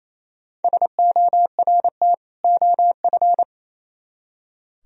Morse code, heard as a keyed tone, about 33 wpm, 700 Hz. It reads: SORT OF
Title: Speed-Racing - Sets of 2 Words 28wpm